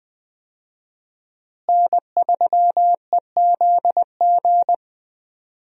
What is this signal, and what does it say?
Morse code, heard as a keyed tone, about 20 wpm, 700 Hz. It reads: N3EZG
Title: USA Call Signs - 5 Characters - Repeated 3x - 20wpm